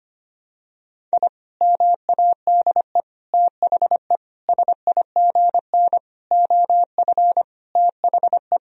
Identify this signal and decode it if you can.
Morse code, heard as a keyed tone, about 25 wpm, 700 Hz. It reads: I MADE THE SIGN OF THE
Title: Sentences from Top 1000 Words - Review Part 1 - 25wpm